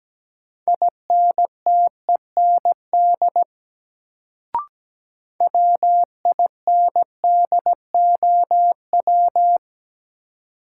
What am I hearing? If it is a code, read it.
Morse code, heard as a keyed tone, about 17 wpm, 700 Hz. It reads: INTEND WINDOW
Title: Sentences from Top 1600 Words - New Word Drill Alt - 17wpm